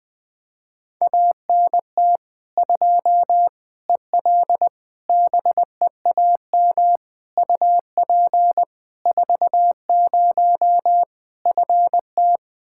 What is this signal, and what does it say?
Morse code, heard as a keyed tone, about 21 wpm, 700 Hz. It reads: ANT 2 EL BEAM UP 40 FT
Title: Practice QSO - Exchange 2 - Info - 20wpm